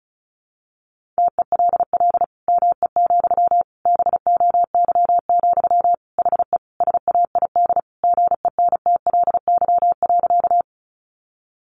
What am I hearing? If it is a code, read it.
Morse code, heard as a keyed tone, about 34 wpm, 700 Hz. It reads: TELL ME, BOY, HE SAID GENTLY.
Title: Sentences from Top 1500 Words 35wpm